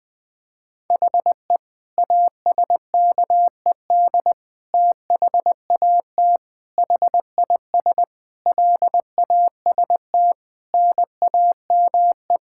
Morse code, heard as a keyed tone, about 21 wpm, 700 Hz. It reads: HE ASKED THAT HIS LAST NAME
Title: Sentences with 9-Word Phrases - Spoken Beginning - 20wpm